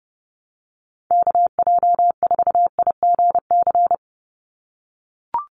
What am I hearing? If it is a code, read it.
Morse code, heard as a keyed tone, about 30 wpm, 700 Hz. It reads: KJ4IGC
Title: USA Call Signs 30wpm